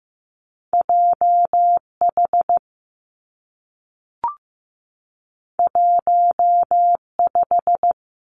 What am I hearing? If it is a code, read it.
Morse code, heard as a keyed tone, about 15 wpm, 700 Hz. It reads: J H 1 5
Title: Two Challenging Letter-Number Combinations 15wpm